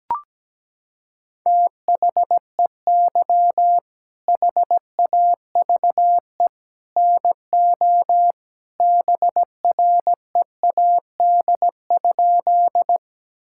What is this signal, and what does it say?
Morse code, heard as a keyed tone, about 17 wpm, 700 Hz. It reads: THEY HAVE NO BREAD?
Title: Sentences from Top 1600 Words 17wpm